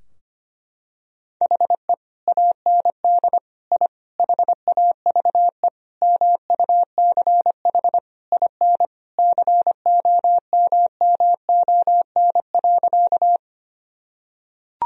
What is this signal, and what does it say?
Morse code, heard as a keyed tone, about 25 wpm, 700 Hz. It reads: HE AND I HAVE MUCH IN COMMON.
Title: Sentences from Top 1000 Words - Review Part 3 - 25wpm